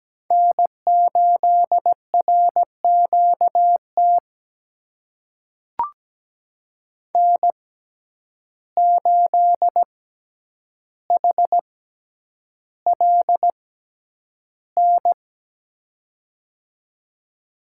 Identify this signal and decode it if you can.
Morse code, heard as a keyed tone, about 17 wpm, spaced as for 9 wpm, 700 Hz. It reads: N8RQT N8HLN
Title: USA Call Signs - 5 Characters - 6x Character Spacing - 17wpm